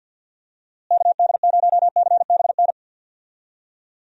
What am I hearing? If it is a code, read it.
Morse code, heard as a keyed tone, about 46 wpm, 700 Hz. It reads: KD0CBN
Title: USA Call Signs - 6 Characters - Repeated 3x - 50wpm